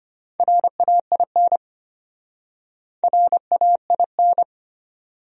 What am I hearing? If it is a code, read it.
Morse code, heard as a keyed tone, about 33 wpm, 700 Hz. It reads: RAIN RAIN